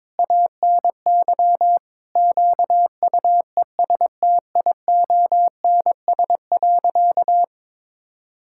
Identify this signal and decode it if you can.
Morse code, heard as a keyed tone, about 22 wpm, 700 Hz. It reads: ANY QUESTIONS.
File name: practice / 22wpm / Sentences with 8-Word Phrases - 22wpm